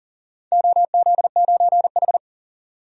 Morse code, heard as a keyed tone, about 40 wpm, 700 Hz. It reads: OZ9H
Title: DXCC-100 Call Signs - 4 Characters - 40wpm